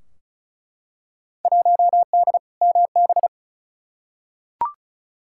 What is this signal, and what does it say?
Morse code, heard as a keyed tone, about 36 wpm, 700 Hz. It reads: 1D MB
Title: Field Day Exchanges Encore - 35wpm